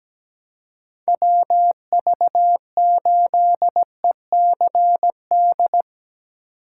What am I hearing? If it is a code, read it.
Morse code, heard as a keyed tone, about 17 wpm, 700 Hz. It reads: WV8ECD
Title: USA Call Signs - 6 Characters - 2x Character Spacing - 17wpm